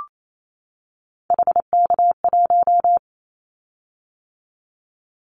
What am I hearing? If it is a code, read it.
Morse code, heard as a keyed tone, about 28 wpm, 700 Hz. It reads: HK1
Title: Callsign Prefixes for Top 100 DXCC - 28wpm